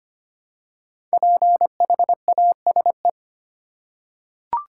Morse code, heard as a keyed tone, about 25 wpm, 700 Hz. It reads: PHASE
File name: 5 Letter Words - First 3 Letters Given - 25wpm